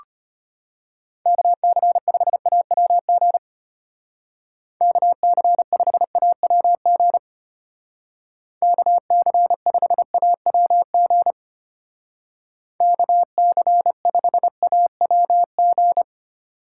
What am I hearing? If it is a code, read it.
Morse code, heard as a keyed tone, about 31 wpm, 700 Hz. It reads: KC5AWG KC5AWG KC5AWG KC5AWG
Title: Speed Racing - USA Call Signs 25wpm